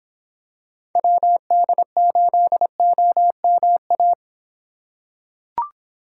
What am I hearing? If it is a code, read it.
Morse code, heard as a keyed tone, about 24 wpm, 700 Hz. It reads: WD8OMA